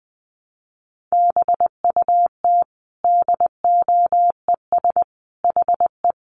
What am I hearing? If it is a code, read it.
Morse code, heard as a keyed tone, about 20 wpm, 700 Hz. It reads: BUT DOES HE
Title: Sentences from Top 1200 Words 20wpm